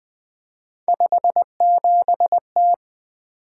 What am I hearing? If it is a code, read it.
Morse code, heard as a keyed tone, about 20 wpm, 700 Hz. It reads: 57T
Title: Three Letter-Number Combinations - 20wpm